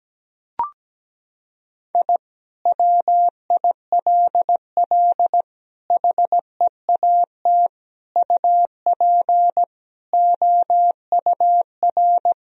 Morse code, heard as a keyed tone, about 17 wpm, 700 Hz. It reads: I WILL HEAT UP OUR
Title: Sentences from Top 1000 Words - Review Part 3 - 17wpm